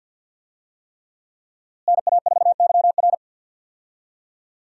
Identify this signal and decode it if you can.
Morse code, heard as a keyed tone, about 50 wpm, 700 Hz. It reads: NA4YR
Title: USA Call Signs - 5 Characters - Repeated 3x - 50wpm